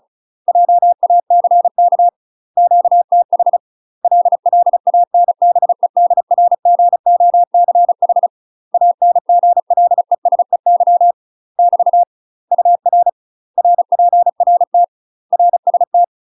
Morse code, heard as a keyed tone, about 35 wpm, 700 Hz. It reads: JACK QTH LLANBEDRGOCH ANGLESEY <BT> UR RPRT RST
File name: UK MASTER QSOs TRAINING / 35WPM / UK MASTER QSOs 0023